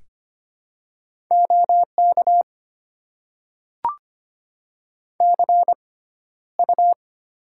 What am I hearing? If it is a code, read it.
Morse code, heard as a keyed tone, about 25 wpm, spaced as for 13 wpm, 690 Hz. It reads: OK CU
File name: Top 90 QSO Elements - 6x Character Spacing - 25wpm